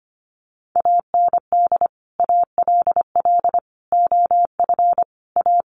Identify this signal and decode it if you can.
Morse code, heard as a keyed tone, about 25 wpm, 700 Hz. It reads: AND ALL OF A
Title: Sets of 5 Words 25wpm